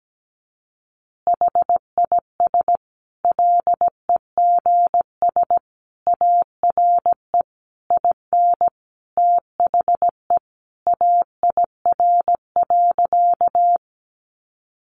Morse code, heard as a keyed tone, about 17 wpm, 700 Hz. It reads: HIS LEGS ARE IN THE AIR.
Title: Additional Easy Phrases 17wpm